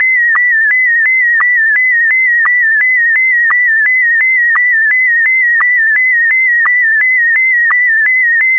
SSTV (slow-scan television), Scottie DX.
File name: sstv / sstv1 / scottie dx